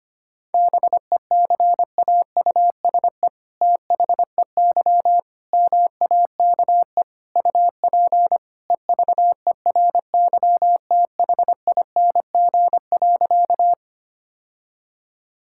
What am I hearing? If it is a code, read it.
Morse code, heard as a keyed tone, about 25 wpm, 700 Hz. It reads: BECAUSE THEY MAKE UP EVERYTHING.
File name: practice / 25wpm / All-Age Appropriate Jokes - 25wpm